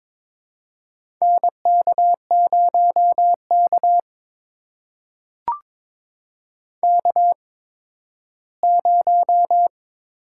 Morse code, heard as a keyed tone, about 21 wpm, spaced as for 13 wpm, 700 Hz. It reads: NK0K K0
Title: USA Call Signs - 8x Character Spacing - 22wpm